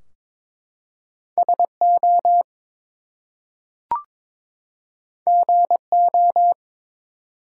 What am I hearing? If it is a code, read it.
Morse code, heard as a keyed tone, about 22 wpm, 700 Hz. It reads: SO GO
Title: Two Letter Words - 22wpm